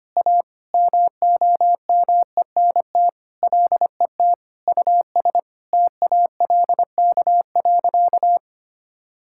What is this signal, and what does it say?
Morse code, heard as a keyed tone, about 25 wpm, 700 Hz. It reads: A MOMENT LET US TALK.
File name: practice / 25wpm / Sentences with 4-Word Phrases - 25wpm